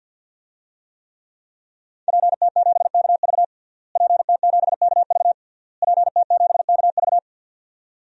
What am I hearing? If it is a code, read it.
Morse code, heard as a keyed tone, about 52 wpm, 700 Hz. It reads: PT7KV PT7KV PT7KV
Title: DXCC-100 Call Signs - 5 Characters - Repeated 3x - 50wpm